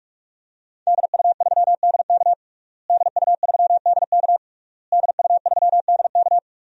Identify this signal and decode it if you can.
Morse code, heard as a keyed tone, about 46 wpm, 700 Hz. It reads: DU3DK DU3DK DU3DK
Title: DXCC-100 Call Signs - 5 Characters - Repeated 3x - 45wpm